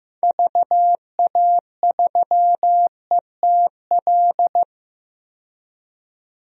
Morse code, heard as a keyed tone, about 15 wpm, 700 Hz. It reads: VA3ETL
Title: DXCC-100 Call Signs - 6 Characters - 15wpm